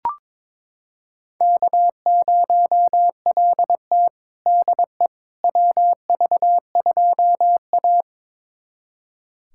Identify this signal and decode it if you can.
Morse code, heard as a keyed tone, about 22 wpm, 700 Hz. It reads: K0LT DE WV2A
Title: USA Callsign DE Callsign - 4 Characters - 22wpm